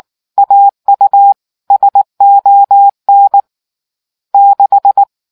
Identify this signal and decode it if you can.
Morse code, heard as a keyed tone, about 19 wpm, 800 Hz. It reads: AU SON 6